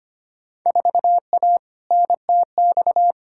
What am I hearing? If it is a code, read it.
Morse code, heard as a keyed tone, about 25 wpm, 700 Hz. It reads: 4A NTX
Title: Field Day Exchanges - 25wpm